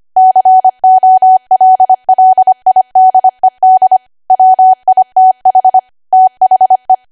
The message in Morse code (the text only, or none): COLLIDED WITH THE